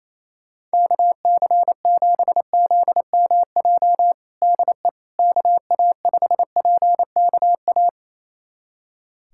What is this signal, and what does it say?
Morse code, heard as a keyed tone, about 28 wpm, 700 Hz. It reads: KC7ZMJ DE KA5PKA
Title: USA Callsign DE Callsign - 6 Characters - 28wpm